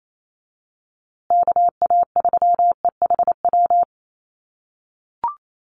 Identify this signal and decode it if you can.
Morse code, heard as a keyed tone, about 28 wpm, 700 Hz. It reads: KA3EHW
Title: USA Call Signs Repeated 3x 28wpm